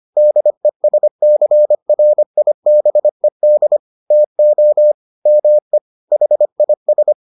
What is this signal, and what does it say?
Morse code, heard as a keyed tone, about 25 wpm, 600 Hz. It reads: DESCRIBED TO ME HIS